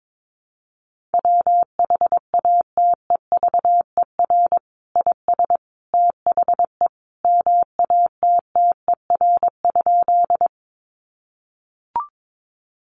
Morse code, heard as a keyed tone, about 22 wpm, 700 Hz. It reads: WHATEVER IS THE MATTER?
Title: Sentences from Top 1100 Words 22wpm